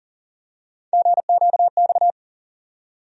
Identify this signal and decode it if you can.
Morse code, heard as a keyed tone, about 40 wpm, 700 Hz. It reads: GQX
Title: USA Call Sign Postfixes - 3 Letters 40wpm